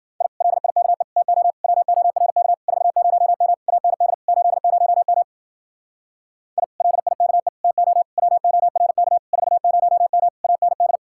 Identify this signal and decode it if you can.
Morse code, heard as a keyed tone, about 51 wpm, 700 Hz. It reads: I LIKE TO WORK 40M AND 80M I LIKE TO WORK 40M AND